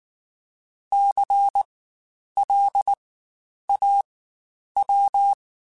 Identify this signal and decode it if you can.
Morse code, heard as a keyed tone, about 19 wpm, spaced as for 9 wpm, 800 Hz. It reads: CLAW
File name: Spaced Alphas / Spaced Alphas mp3s / Cs sp @18